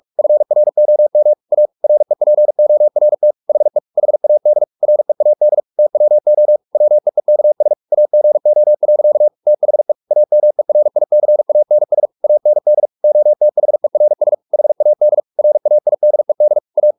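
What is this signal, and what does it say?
Morse code, heard as a keyed tone, about 44 wpm, 600 Hz. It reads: FROM A REPORT HE HAD READ TWO WEEKS AGO. THE AMERICANS AND OTHERS HAD RAIDED A
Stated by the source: Morse practice at 45 wpm, the novel Never by Ken Follett in English, from CW med Gnister.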